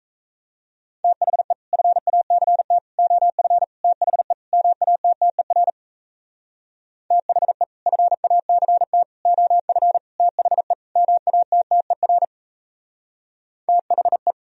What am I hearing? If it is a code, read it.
Morse code, heard as a keyed tone, about 38 wpm, 700 Hz. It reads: THE FACT OF THE MATTER THE FACT OF THE MATTER THE